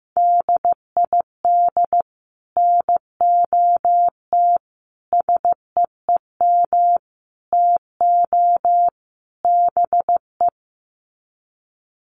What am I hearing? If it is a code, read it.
Morse code, heard as a keyed tone, about 15 wpm, 700 Hz. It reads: DID NOT SEEM TO BE
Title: Sets of 5 Words Encore 15wpm